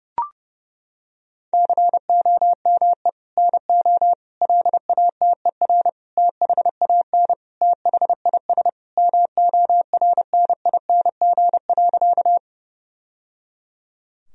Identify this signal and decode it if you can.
Morse code, heard as a keyed tone, about 30 wpm, 700 Hz. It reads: COME NO LATER THAN THIS MORNING.
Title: Sentences from Top 1000 Words - Review Part 4 - 30wpm